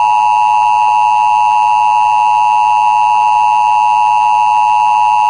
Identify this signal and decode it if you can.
Emergency Alert System (SAME) tones (the 853 and 960 Hz attention tone), with noise. It is heard from 0.5 to 5.0 s.
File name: WHYY